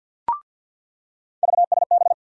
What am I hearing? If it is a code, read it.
Morse code, heard as a keyed tone, about 47 wpm, 690 Hz. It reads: VSB
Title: USA Call Sign Postfixes - 3 Letters 50wpm